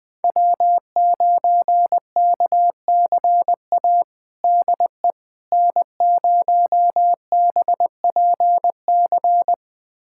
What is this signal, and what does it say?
Morse code, heard as a keyed tone, about 20 wpm, 700 Hz. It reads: W9KCA DE N0BPC
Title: USA Callsign DE Callsign - 5 Characters - 20wpm